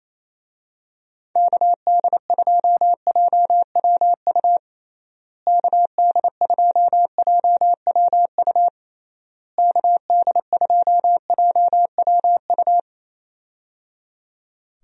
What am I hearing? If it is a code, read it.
Morse code, heard as a keyed tone, about 28 wpm, 700 Hz. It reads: KD2JWU KD2JWU KD2JWU
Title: USA Call Signs - 6 Characters - Repeated 3x - 28wpm